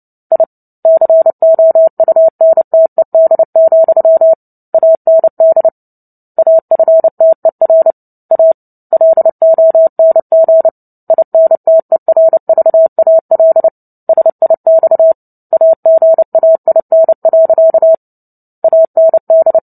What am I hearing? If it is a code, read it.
Morse code, heard as a keyed tone, about 30 wpm, 650 Hz. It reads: I COUNTED, AND AFTER A LONG INTERVAL SIX AGAIN. AND
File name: Book Two / 03-Chapter 3 - 29 WPM